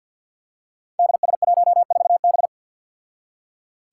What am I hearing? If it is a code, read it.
Morse code, heard as a keyed tone, about 52 wpm, 700 Hz. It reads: DS14B
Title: Five Challenging Letter-Number Combinations 50wpm